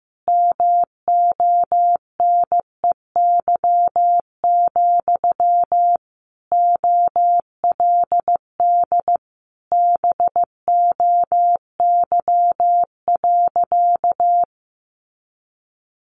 Morse code, heard as a keyed tone, about 15 wpm, 700 Hz. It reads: MONEY, OLD BOY.
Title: Sentences from Top 1500 Words 15wpm